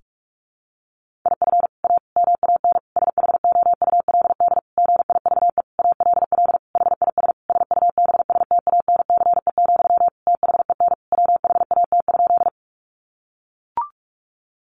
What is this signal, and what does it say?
Morse code, heard as a keyed tone, about 44 wpm, 700 Hz. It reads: IF A MAN SHOULD GIVE ALL HIS SUBSTANCE, THEN WHAT?
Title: Sentences from Top 1500 Words 45wpm